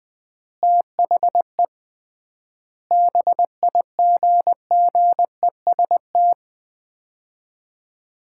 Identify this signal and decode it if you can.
Morse code, heard as a keyed tone, about 20 wpm, 700 Hz. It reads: THE BIGGEST
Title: 3x Word-Spacing - Sets of 4 Words - 20wpm